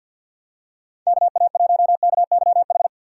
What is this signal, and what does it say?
Morse code, heard as a keyed tone, about 48 wpm, 700 Hz. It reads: KA1KYH
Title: USA Call Signs - 6 Characters - 3x Character Spacing - 50wpm